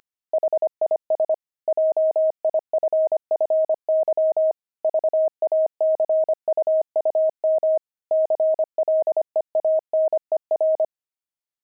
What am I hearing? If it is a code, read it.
Morse code, heard as a keyed tone, about 25 wpm, 630 Hz. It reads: HIS JIFFY VACUUM CLEANER
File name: pangram-fs-25-cs-25-aardvark